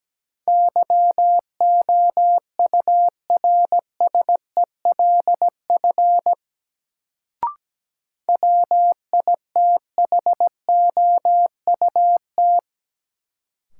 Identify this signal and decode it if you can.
Morse code, heard as a keyed tone, about 17 wpm, 700 Hz. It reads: YOURSELF WITHOUT
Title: Sentences from Top 600 Words - New Word Drill - 17wpm